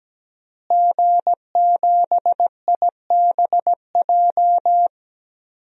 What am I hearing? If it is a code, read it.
Morse code, heard as a keyed tone, about 17 wpm, 700 Hz. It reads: G7IBJ
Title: DXCC-100 Call Signs - 5 Characters - Repeated 3x - 17wpm